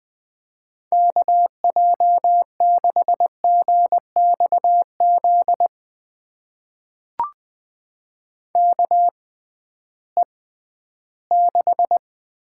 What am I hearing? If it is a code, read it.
Morse code, heard as a keyed tone, about 20 wpm, spaced as for 13 wpm, 700 Hz. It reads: KJ6GXZ KE6
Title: USA Call Signs - 6 Characters - 6x Character Spacing - 20wpm